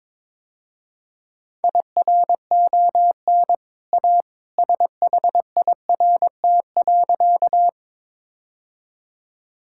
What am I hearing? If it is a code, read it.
Morse code, heard as a keyed tone, about 22 wpm, 700 Hz. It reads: IRON A SHIRT.
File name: practice / 22wpm / Easy Sentences with Hints - 22wpm